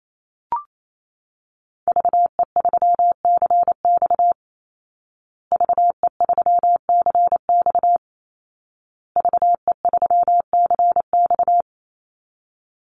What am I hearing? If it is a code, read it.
Morse code, heard as a keyed tone, about 28 wpm, 700 Hz. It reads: VE3CX VE3CX VE3CX
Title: Call Signs Repeated 2x - 28wpm